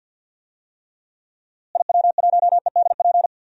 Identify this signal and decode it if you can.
Morse code, heard as a keyed tone, about 52 wpm, 700 Hz. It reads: IW1EDP
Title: DXCC-100 Call Signs - 6 Characters - Repeated 3x - 50wpm